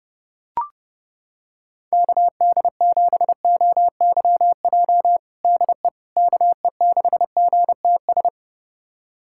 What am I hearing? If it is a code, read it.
Morse code, heard as a keyed tone, about 30 wpm, 700 Hz. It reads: KD7OYJ DE KE6GTS
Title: USA Callsign DE Callsign - 6 Characters - 30wpm